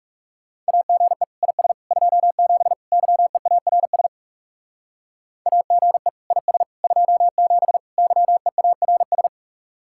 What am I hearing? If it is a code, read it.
Morse code, heard as a keyed tone, about 36 wpm, 700 Hz. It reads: AGE IS 27 YEARS AGE IS 27 YEARS
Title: Speed-Racing Common QSO Phrases - 30wpm